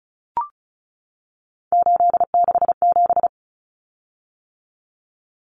Morse code, heard as a keyed tone, about 34 wpm, 700 Hz. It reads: 867
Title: Common Cut Numbers - 3-Digit 35wpm